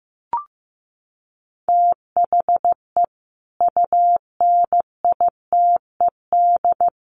Morse code, heard as a keyed tone, about 15 wpm, 700 Hz. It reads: THE UNITED
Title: Sets of 5 Words Encore 15wpm